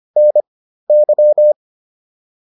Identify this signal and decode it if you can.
Morse code, heard as a keyed tone, about 25 wpm, spaced as for 13 wpm, 600 Hz. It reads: NY
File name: QSO106 13